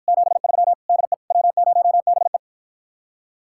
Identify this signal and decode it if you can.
Morse code, heard as a keyed tone, about 54 wpm, 700 Hz. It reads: 73 DE W0BE